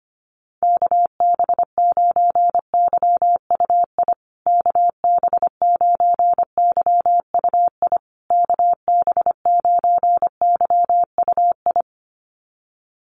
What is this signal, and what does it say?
Morse code, heard as a keyed tone, about 25 wpm, 700 Hz. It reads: KB9YUI KB9YUI KB9YUI
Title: USA Call Signs Repeated 3x 25wpm